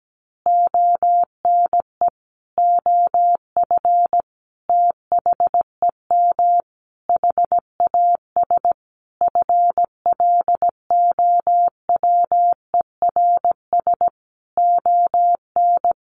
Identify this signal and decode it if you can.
Morse code, heard as a keyed tone, about 17 wpm, 700 Hz. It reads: ONE OF THEM HAS FLOWERS ON
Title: Additional Easy Phrases 17wpm